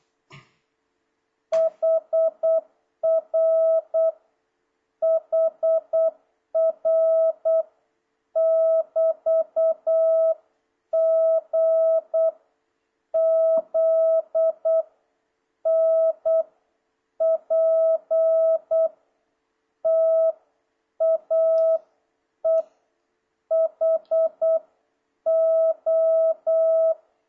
Morse code, heard as a keyed tone, about 8 wpm, 650 Hz. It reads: HR HR <BT> G Z N P T A E H O
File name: jh8cbh / 2026cw